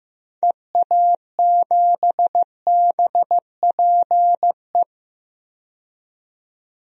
Morse code, heard as a keyed tone, about 14 wpm, 700 Hz. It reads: EA7BPE
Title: DXCC-100 Call Signs - 6 Characters - 15wpm